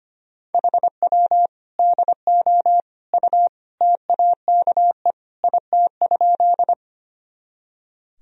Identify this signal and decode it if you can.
Morse code, heard as a keyed tone, about 25 wpm, 700 Hz. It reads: HW DO U TAKE IT?
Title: Sentences from Top 200 Words and Abbreviations 25wpm